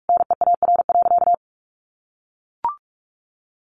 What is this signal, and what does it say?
Morse code, heard as a keyed tone, about 44 wpm, 700 Hz. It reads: NEAR.
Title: Sentences from Top 1500 Words 45wpm